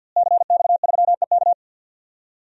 Morse code, heard as a keyed tone, about 52 wpm, 700 Hz. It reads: CX3EK
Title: DXCC-100 Call Signs - 5 Characters - Repeated 2x - 50wpm